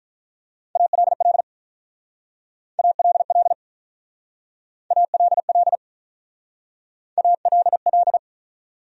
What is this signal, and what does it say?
Morse code, heard as a keyed tone, about 47 wpm, 700 Hz. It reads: ALL ALL ALL ALL